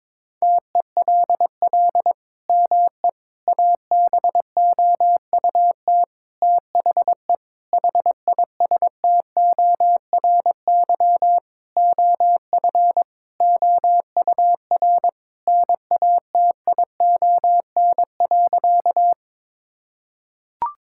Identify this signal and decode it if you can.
Morse code, heard as a keyed tone, about 22 wpm, 700 Hz. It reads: TELL ME ABOUT THE HISTORY OF OUR NATION.
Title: Sentences from Top 1000 Words - Review Part 6 - 22wpm